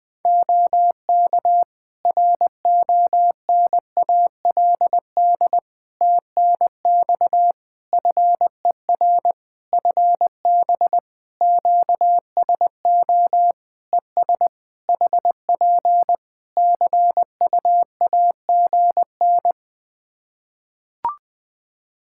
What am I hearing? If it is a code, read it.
Morse code, heard as a keyed tone, about 20 wpm, 700 Hz. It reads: OK RONALD TNX FER FB QSO ES HP CUAGN
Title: Practice QSO - Exchange 4 - Ending - 20wpm